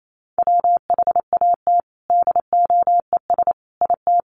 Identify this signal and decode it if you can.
Morse code, heard as a keyed tone, about 28 wpm, 700 Hz. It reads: WHAT DOES IT
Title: Sentences from Top 1300 Words 28wpm